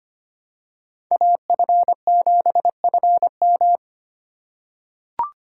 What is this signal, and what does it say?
Morse code, heard as a keyed tone, about 25 wpm, 700 Hz. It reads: AF7FM
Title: USA Call Signs - 3x Character Spacing - 25wpm